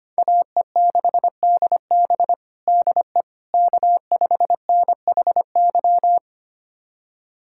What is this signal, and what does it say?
Morse code, heard as a keyed tone, about 25 wpm, 700 Hz. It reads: AE6DB DE K5NHY
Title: USA Callsign DE Callsign - 5 Characters - 25wpm